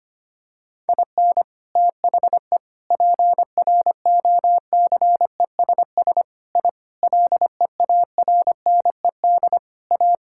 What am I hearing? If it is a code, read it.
Morse code, heard as a keyed tone, about 25 wpm, 700 Hz. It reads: IN THE PROCESS I LEARNED A
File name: practice / 25wpm / Sentences with 3-Word Phrases - 25wpm